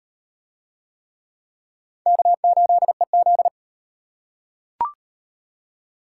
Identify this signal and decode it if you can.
Morse code, heard as a keyed tone, about 38 wpm, 700 Hz. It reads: K8EZ